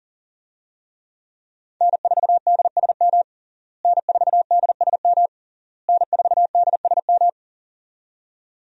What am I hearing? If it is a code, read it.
Morse code, heard as a keyed tone, about 40 wpm, 700 Hz. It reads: N4DSM N4DSM N4DSM
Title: USA Call Signs - 5 Characters - Repeated 3x - 40wpm